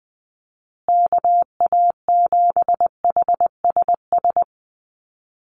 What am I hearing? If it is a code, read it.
Morse code, heard as a keyed tone, about 20 wpm, 700 Hz. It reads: KA7HSS
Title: USA Call Signs Repeated 2x 20wpm